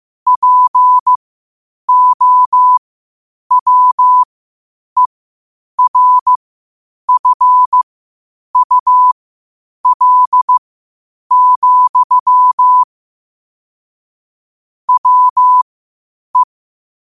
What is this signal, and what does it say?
Morse code, heard as a keyed tone, about 15 wpm, spaced as for 8 wpm, 1000 Hz. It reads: POWERFUL, WE
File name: qotdfiles / 8wpm